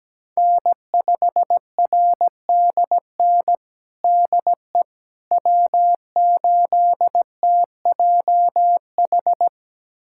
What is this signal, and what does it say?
Morse code, heard as a keyed tone, about 17 wpm, 700 Hz. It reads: N5RDN DE W8TJH
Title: USA Callsign DE Callsign - 5 Characters - 17wpm